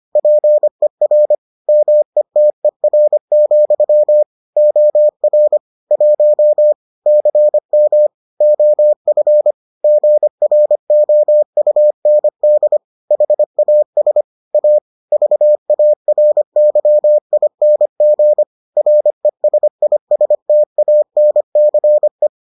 Morse code, heard as a keyed tone, about 25 wpm, 600 Hz. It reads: PER METER, OR 1 CM OF GROUND HAS A VARYING RESISTANCE